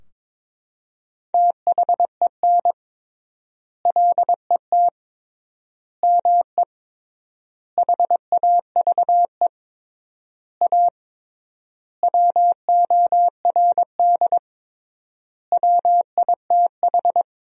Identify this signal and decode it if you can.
Morse code, heard as a keyed tone, about 22 wpm, 700 Hz. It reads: THEN LET ME HAVE A WORD WITH
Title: 3x Word-Spacing - Sentences from Top 300 Words - 22wpm